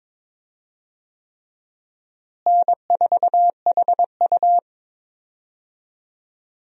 Morse code, heard as a keyed tone, about 21 wpm, 700 Hz. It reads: N4HU
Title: USA Call Signs - 4 Characters - Repeated 3x - 22wpm